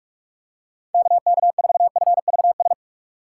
Morse code, heard as a keyed tone, about 44 wpm, 700 Hz. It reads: KK4FVS
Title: USA Call Signs - 6 Characters - 6x Character Spacing - 45wpm